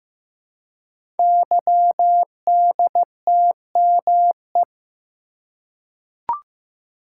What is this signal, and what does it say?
Morse code, heard as a keyed tone, about 15 wpm, 700 Hz. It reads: YDTME
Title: Code Groups - 5 Letters - 15wpm